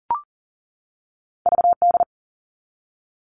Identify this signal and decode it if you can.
Morse code, heard as a keyed tone, about 42 wpm, 700 Hz. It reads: VD
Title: All Cut Numbers - 2-Digit 40wpm